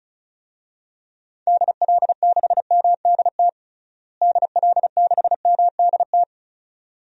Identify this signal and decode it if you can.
Morse code, heard as a keyed tone, about 36 wpm, 700 Hz. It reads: DL6MDT DL6MDT
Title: DXCC-100 Call Signs - 6 Characters - Repeated 2x - 35wpm